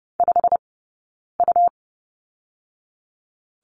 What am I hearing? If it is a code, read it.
Morse code, heard as a keyed tone, about 30 wpm, spaced as for 9 wpm, 700 Hz. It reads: 5U
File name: Lesson 20d - Number 3 - Two Characters 30wpm